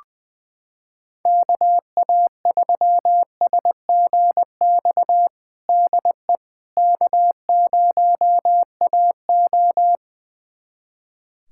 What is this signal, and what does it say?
Morse code, heard as a keyed tone, about 20 wpm, 700 Hz. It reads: KA3SGX DE K0AO
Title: USA Callsign DE Callsign 20wpm